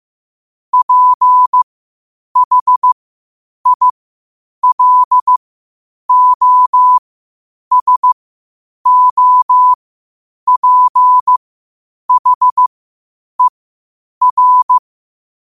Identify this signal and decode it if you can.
Morse code, heard as a keyed tone, about 15 wpm, spaced as for 8 wpm, 1000 Hz. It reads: PHILOSOPHER
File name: qotdfiles / 8wpm